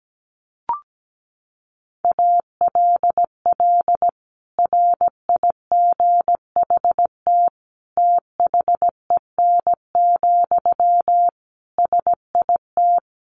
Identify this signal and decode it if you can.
Morse code, heard as a keyed tone, about 17 wpm, 700 Hz. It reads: ALL RIGHT THEN, SIT
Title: Sentences from Top 500 Words 17wpm